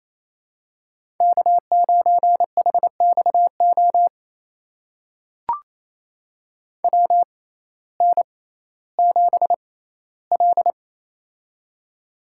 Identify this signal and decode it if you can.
Morse code, heard as a keyed tone, about 28 wpm, spaced as for 15 wpm, 700 Hz. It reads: K9HXO WN7L
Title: USA Call Signs - 6x Character Spacing - 28wpm